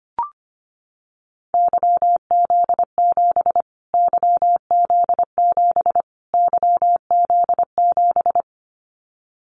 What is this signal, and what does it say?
Morse code, heard as a keyed tone, about 25 wpm, 700 Hz. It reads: YZ7 YZ7 YZ7
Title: Callsign Prefixes for Top 25 DXCC - Repeated 3x - 25wpm